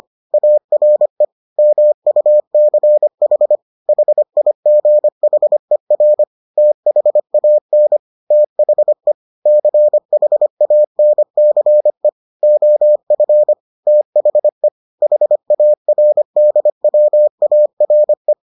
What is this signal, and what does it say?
Morse code, heard as a keyed tone, about 25 wpm, 600 Hz. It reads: ARE MUCH HIGHER THAN THE CHANCE OF THE HARDWARE